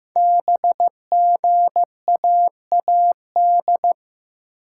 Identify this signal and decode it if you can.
Morse code, heard as a keyed tone, about 15 wpm, 700 Hz. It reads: BGAAD
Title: Code Groups - 5 Letters - 15wpm